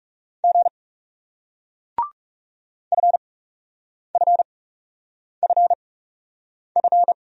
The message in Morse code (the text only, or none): G FFFF